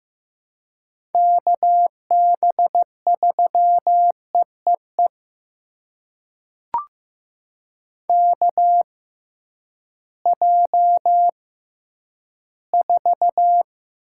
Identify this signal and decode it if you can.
Morse code, heard as a keyed tone, about 15 wpm, spaced as for 10 wpm, 700 Hz. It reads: KB3EEE KJ4
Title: USA Call Signs - 6x Character Spacing - 15wpm